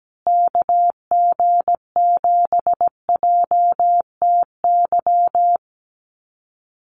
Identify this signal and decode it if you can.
Morse code, heard as a keyed tone, about 17 wpm, 700 Hz. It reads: KG7JTY
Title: USA Call Signs Repeated 2x 17wpm